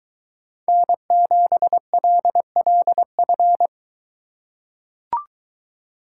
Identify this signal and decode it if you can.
Morse code, heard as a keyed tone, about 22 wpm, 700 Hz. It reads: N7LLF